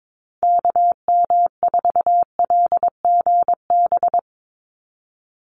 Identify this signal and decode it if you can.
Morse code, heard as a keyed tone, about 22 wpm, 700 Hz. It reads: KM4LGB
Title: USA Call Signs 22wpm